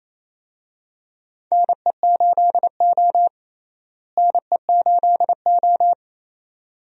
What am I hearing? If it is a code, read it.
Morse code, heard as a keyed tone, about 28 wpm, 700 Hz. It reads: NE8O NE8O
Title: USA Call Signs - 4 Characters - Repeated 2x - 28wpm